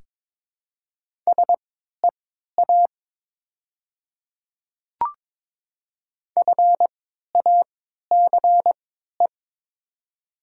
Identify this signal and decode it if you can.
Morse code, heard as a keyed tone, about 23 wpm, spaced as for 13 wpm, 700 Hz. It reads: SEA FACE
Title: 3x Character Spacing - Top 300 Words - 22wpm